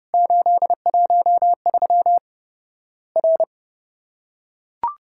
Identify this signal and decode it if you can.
Morse code, heard as a keyed tone, about 30 wpm, 700 Hz. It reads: 813 R
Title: Worked All Europe DX Contest - QTC Exchange - 30wpm